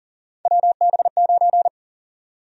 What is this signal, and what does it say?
Morse code, heard as a keyed tone, about 40 wpm, 700 Hz. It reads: WB9
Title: USA Call Sign Prefixes - Repeated 2x - 40wpm